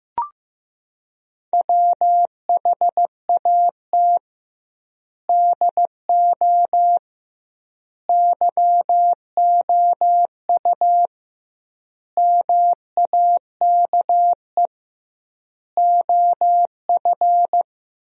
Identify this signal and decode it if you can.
Morse code, heard as a keyed tone, about 15 wpm, 700 Hz. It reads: WHAT DO YOU MAKE OF
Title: 2x Word-Spacing - Sentences from Top 100 Words - 15wpm